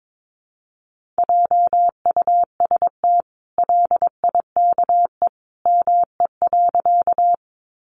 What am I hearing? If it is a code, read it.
Morse code, heard as a keyed tone, about 22 wpm, 700 Hz. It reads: JUST LIKE ME.
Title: Sentences from Top 200 Words 22wpm